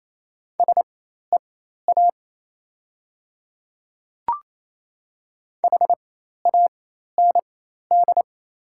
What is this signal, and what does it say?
Morse code, heard as a keyed tone, about 28 wpm, spaced as for 13 wpm, 700 Hz. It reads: SEA HAND
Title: 4x Character Spacing - Top 300 Words - 28wpm